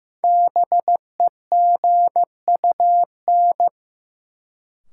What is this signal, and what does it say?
Morse code, heard as a keyed tone, about 15 wpm, 700 Hz. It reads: BEGUN
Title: Sentences from Top 1000 Words - Review Part 6 - New Word Drill - 15wpm